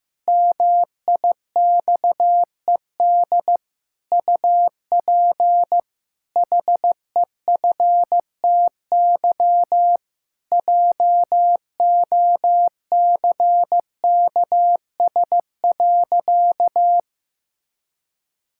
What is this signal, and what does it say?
Morse code, heard as a keyed tone, about 16 wpm, 700 Hz. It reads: MIXED UP HEFTY JOCKS.
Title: Pangram Challenge v2 - 15wpm